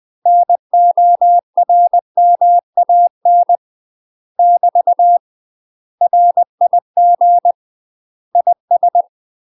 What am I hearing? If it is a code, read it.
Morse code, heard as a keyed tone, about 20 wpm, 700 Hz. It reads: NORMAN <BT> RIG IS